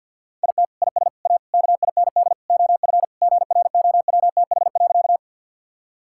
Morse code, heard as a keyed tone, about 52 wpm, 700 Hz. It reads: IT IS A KIND OF GROWTH.
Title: Sentences from Top 1000 Words - Review Part 5 - 50wpm